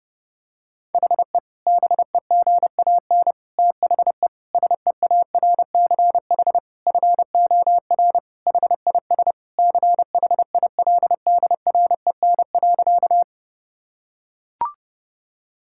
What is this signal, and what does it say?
Morse code, heard as a keyed tone, about 32 wpm, 700 Hz. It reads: HE BEGAN THE SEARCH FOR HIS CHILDREN.
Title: Sentences from Top 1000 Words - Review Part 5 - 30wpm